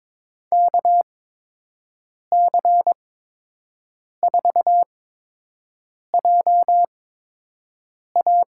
Morse code, heard as a keyed tone, about 22 wpm, spaced as for 6 wpm, 700 Hz. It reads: KC4JA
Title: USA Call Signs - 8x Character Spacing - 22wpm